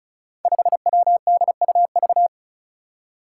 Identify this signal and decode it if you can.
Morse code, heard as a keyed tone, about 36 wpm, 700 Hz. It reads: 5WDUV
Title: Five Challenging Letter-Number Combinations 35wpm